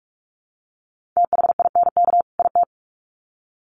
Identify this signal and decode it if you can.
Morse code, heard as a keyed tone, about 44 wpm, 700 Hz. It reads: THINK IT
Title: Sets of 2 Words 45wpm